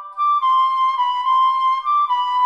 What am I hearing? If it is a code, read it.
There is no code.